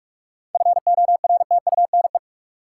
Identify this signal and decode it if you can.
Morse code, heard as a keyed tone, about 46 wpm, 700 Hz. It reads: FORTUNE
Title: Seven Letter Words - 45wpm